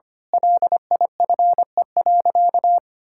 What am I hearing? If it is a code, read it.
Morse code, heard as a keyed tone, about 25 wpm, 700 Hz. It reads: LIFE.
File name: practice / 25wpm / Difficult Sentences with Hints - 25wpm